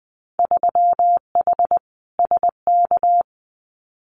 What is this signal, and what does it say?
Morse code, heard as a keyed tone, about 20 wpm, 700 Hz. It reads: 3H SK
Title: Winter Field Day Exchanges 20wpm